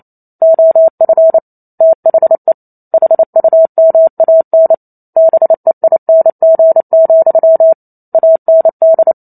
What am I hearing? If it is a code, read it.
Morse code, heard as a keyed tone, about 28 wpm, 650 Hz. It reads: OF THE HUMAN BEING, AND
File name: Chapter 2 - 28 WPM